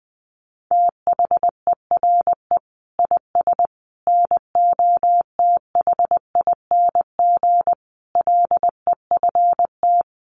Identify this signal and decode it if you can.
Morse code, heard as a keyed tone, about 20 wpm, 700 Hz. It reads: THERE IS NOTHING LEFT
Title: Sentences from Top 1100 Words 20wpm